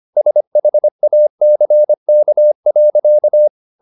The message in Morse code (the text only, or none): SHACK.